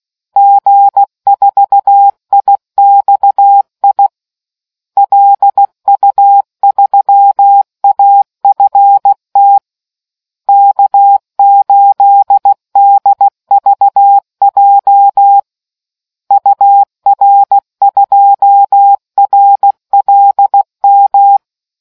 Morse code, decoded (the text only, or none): G4IXI LU3AFT K8DVJ UR2RLM